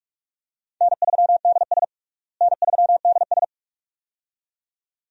Morse code, heard as a keyed tone, about 46 wpm, 700 Hz. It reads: N3DS N3DS
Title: USA Call Signs - 4 Characters - Repeated 2x - 45wpm